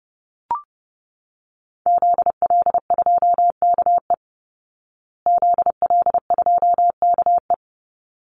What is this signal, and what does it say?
Morse code, heard as a keyed tone, about 30 wpm, 700 Hz. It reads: ZL2KE ZL2KE
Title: Call Signs Repeated 3x - 30wpm